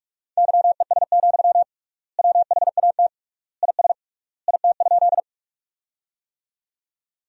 Morse code, heard as a keyed tone, about 46 wpm, 700 Hz. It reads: YES, WHAT IS IT?
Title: 3x Word-Spacing - Sentences from Top 500 Words - 45wpm